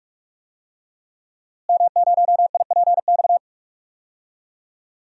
Morse code, heard as a keyed tone, about 46 wpm, 700 Hz. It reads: M0IPX
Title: DXCC-100 Call Signs - 5 Characters - Repeated 2x - 45wpm